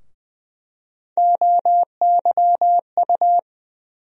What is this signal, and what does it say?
Morse code, heard as a keyed tone, about 20 wpm, 700 Hz. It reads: OYU
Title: USA Call Sign Postfixes - 3 Letters 20wpm